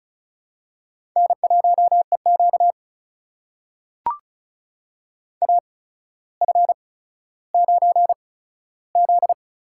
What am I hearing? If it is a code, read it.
Morse code, heard as a keyed tone, about 36 wpm, spaced as for 15 wpm, 700 Hz. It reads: N1EQ AF9Z
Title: USA Call Signs - 4 Characters - 8x Character Spacing - 35wpm